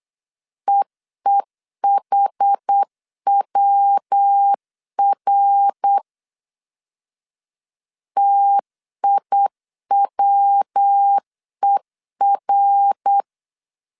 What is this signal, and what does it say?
Morse code, heard as a keyed tone, about 8 wpm, 800 Hz. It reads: EEHWR TIWER